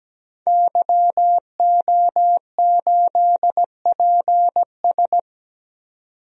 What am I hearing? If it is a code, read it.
Morse code, heard as a keyed tone, about 17 wpm, 700 Hz. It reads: YO8PS
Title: DXCC-100 Call Signs - 5 Characters - 17wpm